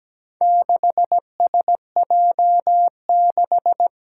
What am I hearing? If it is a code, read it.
Morse code, heard as a keyed tone, about 17 wpm, 700 Hz. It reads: 6SJ6
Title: Four Challenging Letter-Number Combinations 17wpm